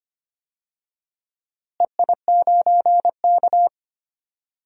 Morse code, heard as a keyed tone, about 25 wpm, 700 Hz. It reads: EI9K
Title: DXCC-100 Call Signs - 4 Characters - Repeated 2x - 25wpm